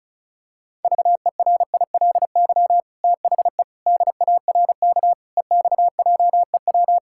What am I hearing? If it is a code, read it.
Morse code, heard as a keyed tone, about 36 wpm, 700 Hz. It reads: VERILY THE DARK EXJEW
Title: Pangram Challenge v2 - 35wpm